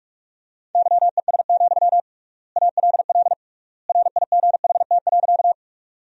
Morse code, heard as a keyed tone, about 46 wpm, 700 Hz. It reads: YES, ALL RIGHT.
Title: 3x Word-Spacing - Sentences from Top 500 Words - 45wpm